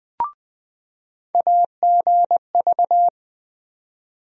Morse code, heard as a keyed tone, about 20 wpm, 700 Hz. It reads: AGV
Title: USA Call Sign Postfixes - 3 Letters 20wpm